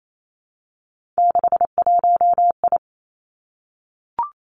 Morse code, heard as a keyed tone, about 28 wpm, 700 Hz. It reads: 61I
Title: Three Challenging Letter-Number Combinations 28wpm